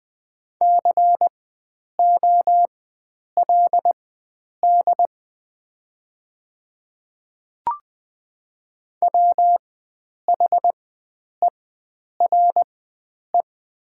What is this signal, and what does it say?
Morse code, heard as a keyed tone, about 20 wpm, spaced as for 9 wpm, 700 Hz. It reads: COLD WHERE
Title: 4x Character Spacing - Top 500 Words - 20wpm